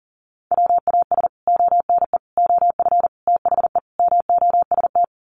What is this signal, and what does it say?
Morse code, heard as a keyed tone, about 40 wpm, 700 Hz. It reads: WAS ONE OF THE MOST
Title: Sets of 5 Words 40wpm